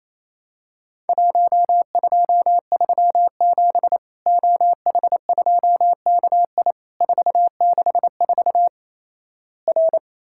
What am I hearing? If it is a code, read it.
Morse code, heard as a keyed tone, about 27 wpm, 700 Hz. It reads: 1237 OH2KI 464 R
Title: Worked All Europe DX Contest - QTC Exchange - 28wpm